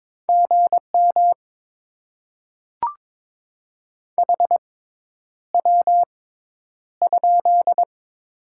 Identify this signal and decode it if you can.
Morse code, heard as a keyed tone, about 22 wpm, spaced as for 10 wpm, 700 Hz. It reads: GM HW?
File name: Top 20 QSO Elements - 6x Character Spacing - 22wpm